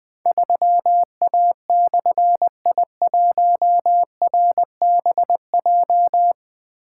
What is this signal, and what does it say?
Morse code, heard as a keyed tone, about 20 wpm, 700 Hz. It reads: 3A/I1RBJ
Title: Difficult Call Signs 20wpm